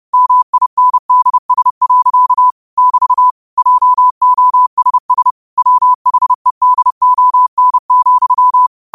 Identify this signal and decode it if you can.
Morse code, heard as a keyed tone, about 30 wpm, 1000 Hz. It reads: MINDS. <BT> JOSS WHEDON,
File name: qotdfiles / 30wpm